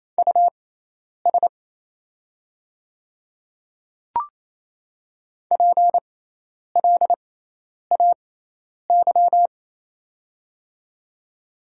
Morse code, heard as a keyed tone, about 28 wpm, spaced as for 10 wpm, 700 Hz. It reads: US PLAY
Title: 6x Character Spacing - Top 200 Words - 28wpm